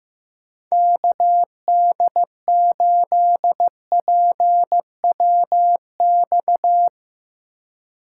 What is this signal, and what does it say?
Morse code, heard as a keyed tone, about 15 wpm, 700 Hz. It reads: KD8PWX
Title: USA Call Signs - 2x Character Spacing - 15wpm